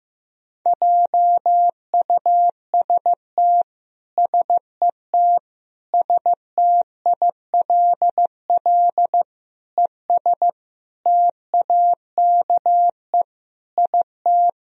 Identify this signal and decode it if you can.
Morse code, heard as a keyed tone, about 15 wpm, 700 Hz. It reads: JUST SET STILL ES TAKE IT
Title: Sentences from Top 300 Words and Abbreviations 15wpm